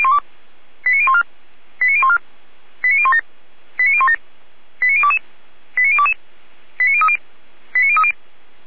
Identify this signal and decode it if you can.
Five-tone selective call (ZVEI), heard as quick runs of five tones, tones of about 70 ms. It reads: 89014 89014 89017 89017 89022 89022 89030 89030